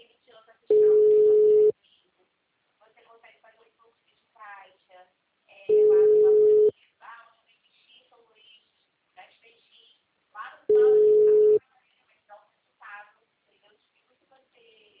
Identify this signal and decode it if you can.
Ringback tone (Europe, 425 Hz), with speech.